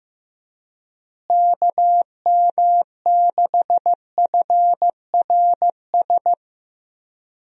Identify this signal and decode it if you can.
Morse code, heard as a keyed tone, about 15 wpm, 700 Hz. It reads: KM6FRS
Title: USA Call Signs - 6 Characters - Repeated 3x - 15wpm